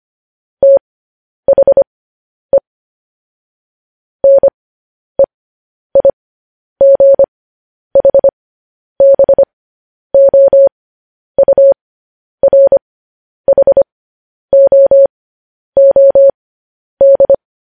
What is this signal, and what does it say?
Morse code, heard as a keyed tone, about 25 wpm, spaced as for 10 wpm, 550 Hz. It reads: THE NEIGHBOURHOOD
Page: news headlines in morse code at 10 wpm